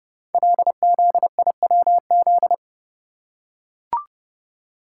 Morse code, heard as a keyed tone, about 30 wpm, 700 Hz. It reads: LZIWZ
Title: Code Groups - 5 Letters - 30wpm